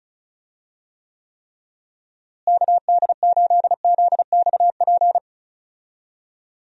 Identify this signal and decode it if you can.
Morse code, heard as a keyed tone, about 36 wpm, 700 Hz. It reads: KD8ZXP
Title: USA Call Signs - 6 Characters - Repeated 3x - 35wpm